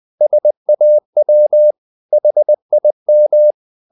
Morse code, heard as a keyed tone, about 20 wpm, 600 Hz. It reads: SAW HIM
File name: the old man and the sea part 2 20wpm